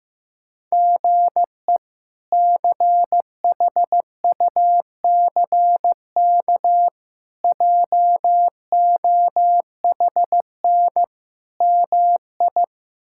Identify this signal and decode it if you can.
Morse code, heard as a keyed tone, about 15 wpm, 700 Hz. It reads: GE CHUCK JOHN MI
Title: K1USN SST QSO Practice - 15wpm